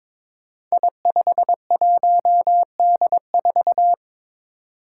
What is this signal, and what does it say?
Morse code, heard as a keyed tone, about 22 wpm, 700 Hz. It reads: I51D4
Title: Code Groups - 5 Letter:Numbers - 22wpm